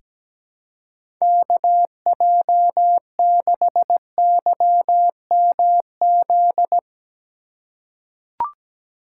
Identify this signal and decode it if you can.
Morse code, heard as a keyed tone, about 17 wpm, 700 Hz. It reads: KJ6YMZ
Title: USA Call Signs - 6 Characters - 3x Character Spacing - 17wpm